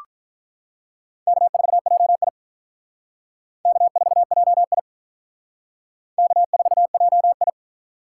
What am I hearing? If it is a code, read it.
Morse code, heard as a keyed tone, about 45 wpm, 700 Hz. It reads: K4JI K4JI K4JI